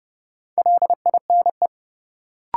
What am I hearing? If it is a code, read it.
Morse code, heard as a keyed tone, about 32 wpm, 700 Hz. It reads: LINE
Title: Sentences from Top 1000 Words - Review Part 1 - New Word Drill - 30wpm